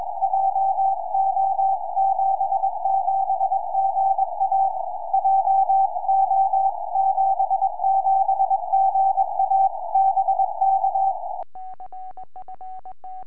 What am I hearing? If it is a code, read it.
Morse code, heard as a keyed tone, about 22 wpm, 780 Hz. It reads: JO77GA JO77GA BK
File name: tropo / sm7gvf 70cm